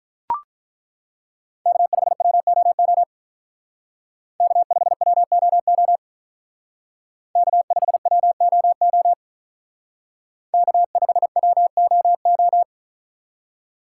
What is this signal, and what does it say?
Morse code, heard as a keyed tone, about 42 wpm, 700 Hz. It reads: K5WOO K5WOO K5WOO K5WOO
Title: Speed Racing - USA 1x3 Call Signs 35wpm